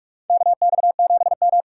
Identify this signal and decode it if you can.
Morse code, heard as a keyed tone, about 42 wpm, 700 Hz. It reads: KX7M
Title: K1USN SST QSO Practice - 45wpm